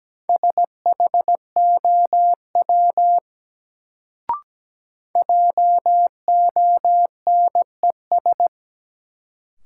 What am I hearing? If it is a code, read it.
Morse code, heard as a keyed tone, about 17 wpm, 700 Hz. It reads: SHOW JONES
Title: Sets of 5 Words - New Word Drill - 17wpm